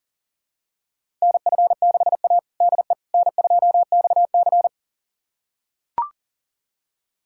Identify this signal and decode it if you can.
Morse code, heard as a keyed tone, about 40 wpm, 700 Hz. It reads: NF6A DE N2XC
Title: USA Callsign DE Callsign - 4 Characters - 40wpm